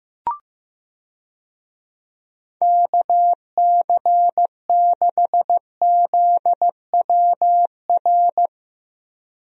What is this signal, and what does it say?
Morse code, heard as a keyed tone, about 15 wpm, 700 Hz. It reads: KC6ZWR
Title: USA Call Signs - 6 Characters - Repeated 3x - 15wpm